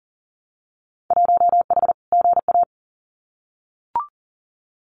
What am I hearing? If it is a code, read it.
Morse code, heard as a keyed tone, about 38 wpm, 700 Hz. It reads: 1H GA
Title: Winter Field Day Exchanges 40wpm